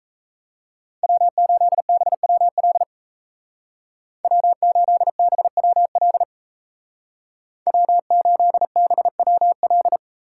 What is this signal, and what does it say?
Morse code, heard as a keyed tone, about 34 wpm, 700 Hz. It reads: W8BWL W8BWL W8BWL
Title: Speed Racing - USA 1x3 Call Signs 28wpm